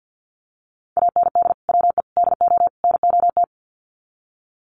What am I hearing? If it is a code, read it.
Morse code, heard as a keyed tone, about 52 wpm, 700 Hz. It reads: AND WE DO NOT
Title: Sets of 4 Words 50wpm